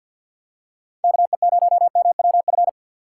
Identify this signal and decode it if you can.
Morse code, heard as a keyed tone, about 51 wpm, 700 Hz. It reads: KE0MWF
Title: USA Call Signs - 6 Characters - 3x Character Spacing - 50wpm